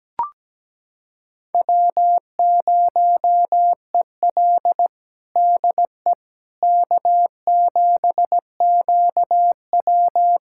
Morse code, heard as a keyed tone, about 17 wpm, 700 Hz. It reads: W0EL DE K7QW
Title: USA Callsign DE Callsign - 4 Characters - 17wpm